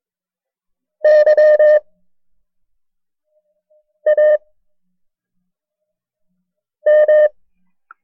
Morse code, heard as a keyed tone, about 22 wpm, 610 Hz. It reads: Y A M
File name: FreeFiles / CW-practice / KB6MT-tape1-side2